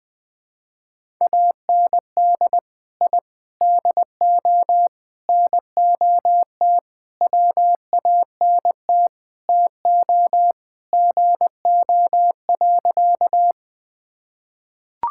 Morse code, heard as a keyed tone, about 20 wpm, 700 Hz. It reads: AND I DO NOT WANT TO GO.
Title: Sentences with 5-Word Phrases - 20wpm